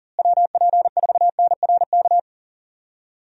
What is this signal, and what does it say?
Morse code, heard as a keyed tone, about 40 wpm, 700 Hz. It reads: WP4NRK
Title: USA Call Signs - 6 Characters - Repeated 3x - 40wpm